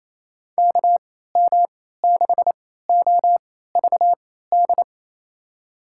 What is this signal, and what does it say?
Morse code, heard as a keyed tone, about 28 wpm, spaced as for 16 wpm, 700 Hz. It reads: KM6OVD
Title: USA Call Signs - 3x Character Spacing - 28wpm